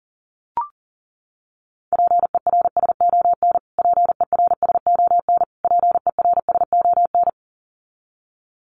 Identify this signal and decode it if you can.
Morse code, heard as a keyed tone, about 37 wpm, 700 Hz. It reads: PERSON PERSON PERSON
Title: Top 500 Words Repeated 40wpm